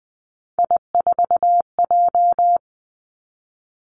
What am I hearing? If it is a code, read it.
Morse code, heard as a keyed tone, about 19 wpm, 700 Hz. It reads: I4J
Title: Three Challenging Letter-Number Combinations 20wpm